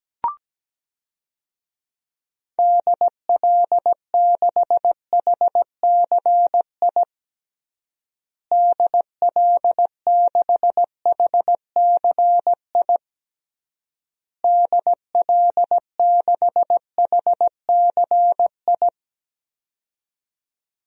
Morse code, heard as a keyed tone, about 17 wpm, 700 Hz. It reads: DL6HCI DL6HCI DL6HCI
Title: DXCC-100 Call Signs - 6 Characters - Repeated 3x - 17wpm